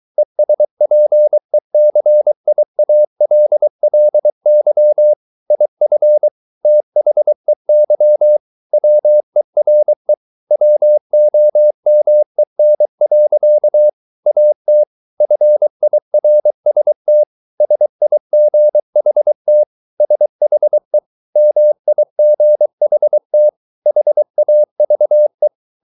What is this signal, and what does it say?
Morse code, heard as a keyed tone, about 23 wpm, 600 Hz. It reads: ESPECIALLY IF THEY WERE WOMEN. AT FIRST SIGHT SHE MIGHT HAVE
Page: Romanen Never af Ken Follet på engelsk.